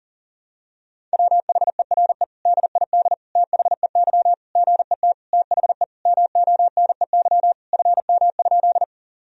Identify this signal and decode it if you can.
Morse code, heard as a keyed tone, about 40 wpm, 700 Hz. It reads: WHERE DID THEY GET THE MONEY FM?
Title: Sentences from Top 400 Words and Abbreviations 40wpm